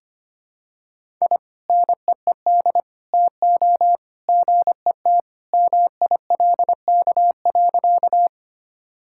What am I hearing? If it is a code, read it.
Morse code, heard as a keyed tone, about 25 wpm, 700 Hz. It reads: I NEED TO GET MILK.
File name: practice / 25wpm / Sentences with 2-Word Phrases - 25wpm